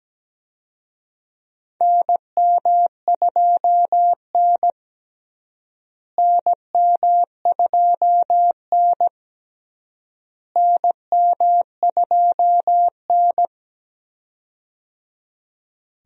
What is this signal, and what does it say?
Morse code, heard as a keyed tone, about 17 wpm, 700 Hz. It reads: NM2N NM2N NM2N
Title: USA Call Signs - 4 Characters - Repeated 3x - 17wpm